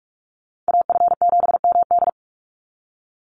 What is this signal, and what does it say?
Morse code, heard as a keyed tone, about 44 wpm, 700 Hz. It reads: AF7MD
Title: NA SOTA Chaser Callsigns 45wpm